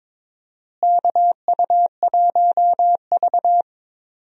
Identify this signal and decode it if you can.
Morse code, heard as a keyed tone, about 22 wpm, 700 Hz. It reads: KU1V
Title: USA Call Signs - 4 Characters - Repeated 3x - 22wpm